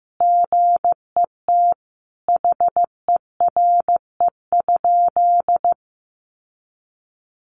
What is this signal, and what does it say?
Morse code, heard as a keyed tone, about 15 wpm, 700 Hz. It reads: GET HERE?
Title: Sentences from Top 1400 Words 15wpm